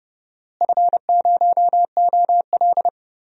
Morse code, heard as a keyed tone, about 32 wpm, 700 Hz. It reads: F0OL
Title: DXCC-100 Call Signs - 4 Characters - Repeated 3x - 30wpm